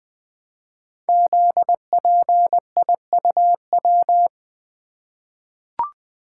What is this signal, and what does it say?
Morse code, heard as a keyed tone, about 20 wpm, 700 Hz. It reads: ZPIUW
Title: Code Groups - 5 Letters - 20wpm